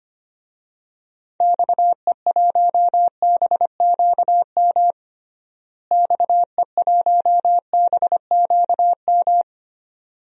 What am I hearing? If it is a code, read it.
Morse code, heard as a keyed tone, about 25 wpm, 700 Hz. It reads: XE1BQM XE1BQM
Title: DXCC-100 Call Signs - 6 Characters - Repeated 2x - 25wpm